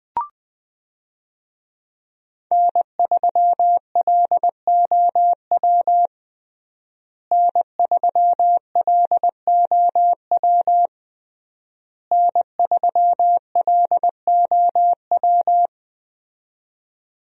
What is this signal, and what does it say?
Morse code, heard as a keyed tone, about 21 wpm, 700 Hz. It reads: N3LOW N3LOW N3LOW
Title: USA Call Signs - 5 Characters - Repeated 3x - 20wpm